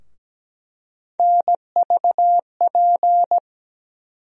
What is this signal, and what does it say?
Morse code, heard as a keyed tone, about 17 wpm, 700 Hz. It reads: NVP
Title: USA Call Sign Postfixes - 3 Letters 17wpm